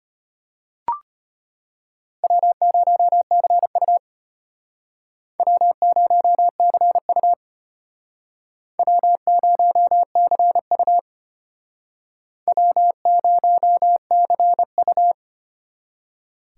Morse code, heard as a keyed tone, about 33 wpm, 700 Hz. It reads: W0CU W0CU W0CU W0CU